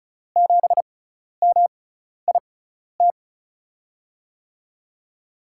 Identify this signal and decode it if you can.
Morse code, heard as a keyed tone, about 36 wpm, spaced as for 12 wpm, 700 Hz. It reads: 7MIT
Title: USA Call Signs - 5 Characters - 6x Character Spacing - 35wpm